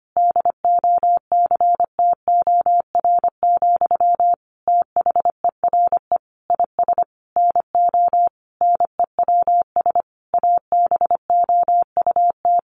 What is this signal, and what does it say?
Morse code, heard as a keyed tone, about 25 wpm, 700 Hz. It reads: DOCTOR, THERE IS NO NEWS ABOUT
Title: Sentences from Top 1500 Words 25wpm